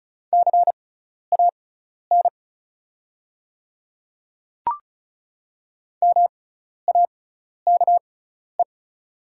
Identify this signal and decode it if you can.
Morse code, heard as a keyed tone, about 34 wpm, spaced as for 12 wpm, 690 Hz. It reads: CAN MAKE
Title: 6x Character Spacing - Top 100 Words - 35wpm